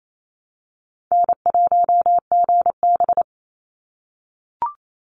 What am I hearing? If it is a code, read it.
Morse code, heard as a keyed tone, about 28 wpm, 700 Hz. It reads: N1GB
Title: NA SOTA Chaser Callsigns 28wpm